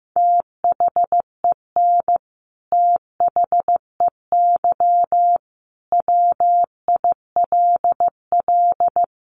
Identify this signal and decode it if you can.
Morse code, heard as a keyed tone, about 15 wpm, 700 Hz. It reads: THEN THEY WILL
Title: Sentences from Top 200 Words 15wpm